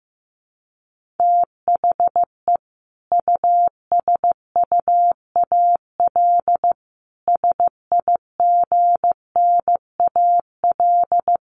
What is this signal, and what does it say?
Morse code, heard as a keyed tone, about 14 wpm, 700 Hz. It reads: THE USUAL SIGNAL
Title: Sentences from Top 1400 Words 15wpm